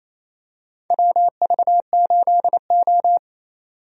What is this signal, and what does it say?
Morse code, heard as a keyed tone, about 28 wpm, 700 Hz. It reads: WV8O
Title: USA Call Signs - 4 Characters - 28wpm